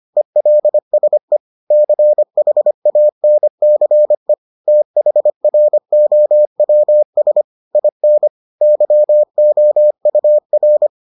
Morse code, heard as a keyed tone, about 25 wpm, 600 Hz. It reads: ELSE CHANCE THROWS IN YOUR